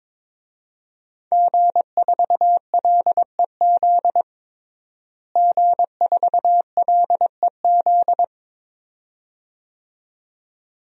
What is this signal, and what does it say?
Morse code, heard as a keyed tone, about 22 wpm, 700 Hz. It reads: G4LEZ G4LEZ
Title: DXCC-100 Call Signs - 5 Characters - Repeated 2x - 22wpm